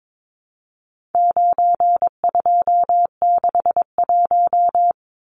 Morse code, heard as a keyed tone, about 22 wpm, 700 Hz. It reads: 9261
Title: Numbers - 4-Digit 22wpm